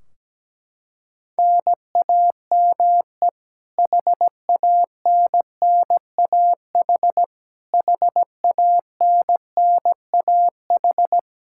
Morse code, heard as a keyed tone, about 17 wpm, 700 Hz. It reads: NAME HANNAH HANNAH
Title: Practice QSO - Exchange 1 - Beginning - 17wpm